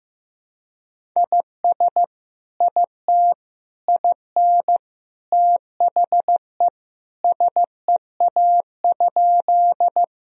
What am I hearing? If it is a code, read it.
Morse code, heard as a keyed tone, about 15 wpm, 700 Hz. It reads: IS IT IN THE SEA?
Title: Sentences from Top 1000 Words - Review Part 1 - 15wpm